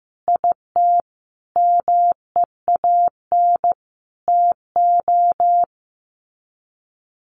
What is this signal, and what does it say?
Morse code, heard as a keyed tone, about 15 wpm, 700 Hz. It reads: IT MEAN TO
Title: Sets of 5 Words Encore2 15wpm